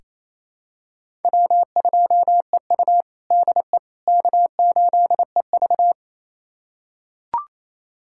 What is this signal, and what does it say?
Morse code, heard as a keyed tone, about 27 wpm, 700 Hz. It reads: W2EU DE K8EV
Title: USA Callsign DE Callsign - 4 Characters - 28wpm